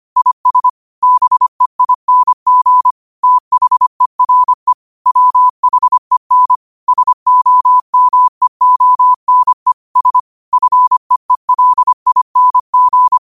Morse code, heard as a keyed tone, about 25 wpm, 1000 Hz. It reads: IS BEING THERE WHEN SOMEONE S FEELING
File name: qotdfiles / 25wpm